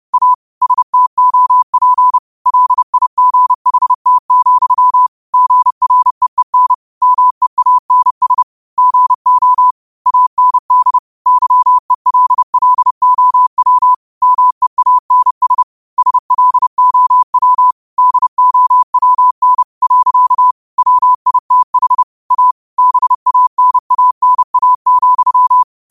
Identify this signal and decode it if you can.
Morse code, heard as a keyed tone, about 30 wpm, 1000 Hz. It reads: A STOP LIGHT, GREEN MEANS GO AND YELLOW MEANS SLOW DOWN. WITH A BANANA,